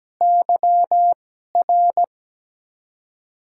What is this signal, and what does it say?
Morse code, heard as a keyed tone, about 17 wpm, spaced as for 12 wpm, 700 Hz. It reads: YR
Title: USA Call Signs - 2x Character Spacing - 17wpm